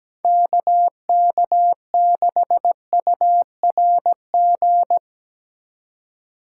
Morse code, heard as a keyed tone, about 20 wpm, 700 Hz. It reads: KK6URG
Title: Speed Racing - USA 2x3 Call Signs 17wpm